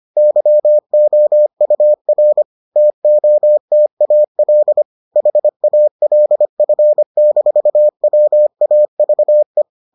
Morse code, heard as a keyed tone, about 25 wpm, 600 Hz. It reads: YOUR TOTAL HALF-WAVE